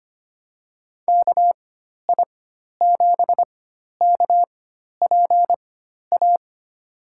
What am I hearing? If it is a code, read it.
Morse code, heard as a keyed tone, about 25 wpm, spaced as for 12 wpm, 700 Hz. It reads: KI7KPA
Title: USA Call Signs - 6 Characters - 4x Character Spacing - 25wpm